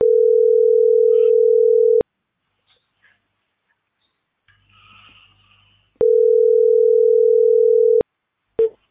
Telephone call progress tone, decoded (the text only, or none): ringback tone